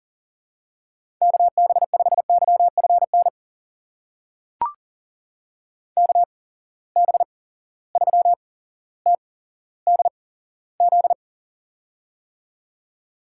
Morse code, heard as a keyed tone, about 42 wpm, spaced as for 17 wpm, 700 Hz. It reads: KB5YFN KB3TDZ
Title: USA Call Signs - 6 Characters - 8x Character Spacing - 40wpm